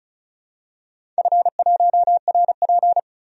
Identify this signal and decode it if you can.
Morse code, heard as a keyed tone, about 36 wpm, 700 Hz. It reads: F1RP
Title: DXCC-100 Call Signs - 4 Characters - 35wpm